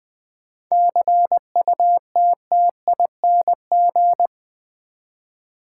Morse code, heard as a keyed tone, about 20 wpm, 700 Hz. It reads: CUTTING
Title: 7 Letter Words - First 3 Letters Given - 20wpm